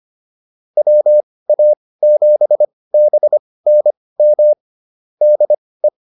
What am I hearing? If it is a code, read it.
Morse code, heard as a keyed tone, about 25 wpm, spaced as for 18 wpm, 600 Hz. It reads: WA7BNM DE
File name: QSO106 18